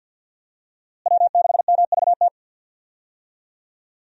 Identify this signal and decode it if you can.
Morse code, heard as a keyed tone, about 47 wpm, 700 Hz. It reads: W6MVT
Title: USA Call Signs - 5 Characters - Repeated 2x - 50wpm